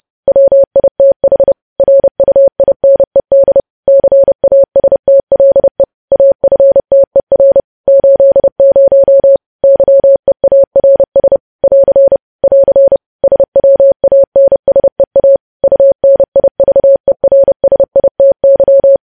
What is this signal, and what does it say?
Morse code, heard as a keyed tone, about 30 wpm, 550 Hz. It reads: WITH RUINED CASTLE AFTER 80 YEARS <AR> <AR> SWANSEA UNIVERSITY
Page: news headlines in morse code at 30 wpm